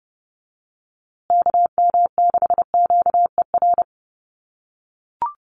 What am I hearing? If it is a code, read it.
Morse code, heard as a keyed tone, about 30 wpm, 700 Hz. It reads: KM6QER
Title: USA Call Signs Repeated 3x 30wpm